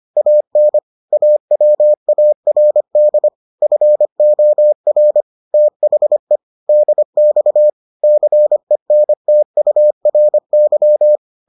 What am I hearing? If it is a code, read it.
Morse code, heard as a keyed tone, about 25 wpm, 600 Hz. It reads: AN AWARD FOR THE DX CENTURY